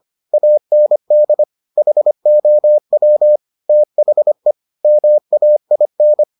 Morse code, heard as a keyed tone, about 25 wpm, 600 Hz. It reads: AND HOW THE MAIN